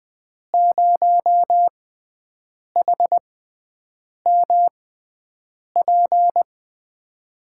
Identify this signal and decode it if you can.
Morse code, heard as a keyed tone, about 20 wpm, spaced as for 7 wpm, 700 Hz. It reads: 0HMP
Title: USA Call Signs - 6 Characters - 6x Character Spacing - 20wpm